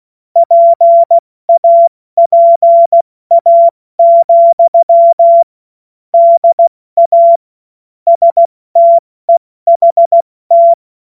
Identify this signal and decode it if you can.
Morse code, heard as a keyed tone, about 16 wpm, 680 Hz. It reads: PAPA, DA STEHT
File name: Morsekurs / Witze / Witz 24. Januar 2023 70 F80